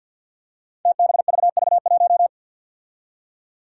Morse code, heard as a keyed tone, about 52 wpm, 700 Hz. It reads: TBVV1
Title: Five Challenging Letter-Number Combinations 50wpm